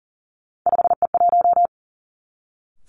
Morse code, heard as a keyed tone, about 41 wpm, 700 Hz. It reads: 5E1
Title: Three Challenging Letter-Number Combinations 40wpm